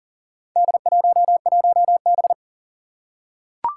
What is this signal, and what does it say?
Morse code, heard as a keyed tone, about 38 wpm, 700 Hz. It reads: D11B
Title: Four Challenging Letter-Number Combinations 40wpm